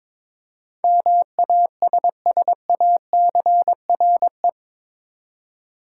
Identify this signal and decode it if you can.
Morse code, heard as a keyed tone, about 22 wpm, 700 Hz. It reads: MASSACRE
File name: practice / 22wpm / Eight Letter Words - 22wpm